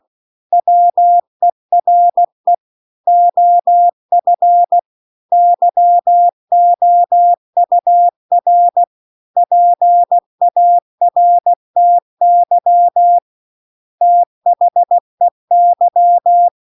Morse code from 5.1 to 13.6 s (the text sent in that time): YOUR PARTY